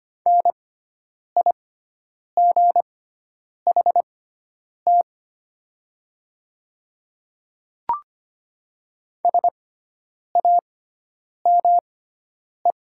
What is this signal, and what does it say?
Morse code, heard as a keyed tone, about 25 wpm, spaced as for 9 wpm, 700 Hz. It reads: NIGHT SAME
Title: 6x Character Spacing - Top 300 Words - 25wpm